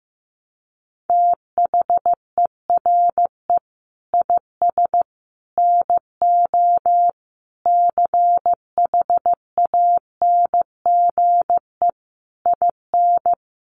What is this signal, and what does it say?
Morse code, heard as a keyed tone, about 15 wpm, 700 Hz. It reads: THERE IS NO CHANGE IN
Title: Sentences from Top 1300 Words 15wpm